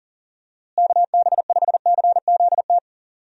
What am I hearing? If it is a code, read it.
Morse code, heard as a keyed tone, about 37 wpm, 700 Hz. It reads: KB5CZT
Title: USA Call Signs - 6 Characters - Repeated 3x - 40wpm